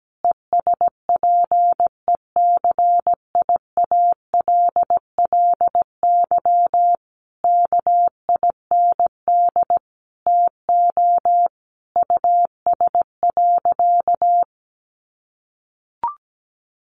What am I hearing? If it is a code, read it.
Morse code, heard as a keyed tone, about 17 wpm, 700 Hz. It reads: ESPECIALLY KIND TO US.
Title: Sentences from Top 700 Words 17wpm